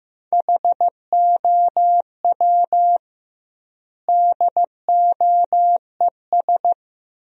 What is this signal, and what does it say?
Morse code, heard as a keyed tone, about 15 wpm, 700 Hz. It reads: HOW DOES
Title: 2x Word-Spacing - Sentences from Top 200 Words - 15wpm